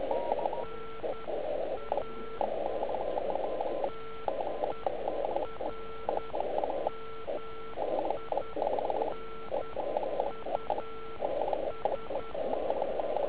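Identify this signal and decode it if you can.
Morse code, heard as a keyed tone, about 10 wpm, spaced as for 8 wpm, 1340 Hz. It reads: NA TERMINUS
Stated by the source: Slow (QRS) CW test by beginner OK1IFW during beginner Morse skeds on 3559.5 kHz (80 m).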